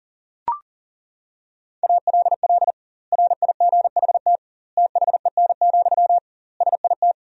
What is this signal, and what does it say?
Morse code, heard as a keyed tone, about 37 wpm, 700 Hz. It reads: ALL RIGHT THEN, SIT
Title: 2x Word-Spacing - Sentences from Top 500 Words - 40wpm